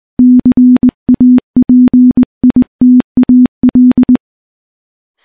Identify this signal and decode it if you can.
Morse code, heard as a keyed tone, about 20 wpm, 250 Hz. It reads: CAPITAL
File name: morse / morse-training-20WPM